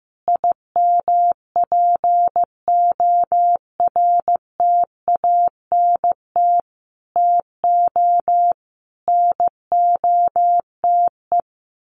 Morse code, heard as a keyed tone, about 15 wpm, 700 Hz. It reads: IMPORTANT TO NOTE
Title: Sets of 5 Words 15wpm